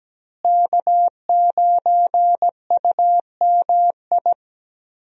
Morse code, heard as a keyed tone, about 17 wpm, 700 Hz. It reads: K9UMI
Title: USA Call Signs - 5 Characters - Repeated 3x - 17wpm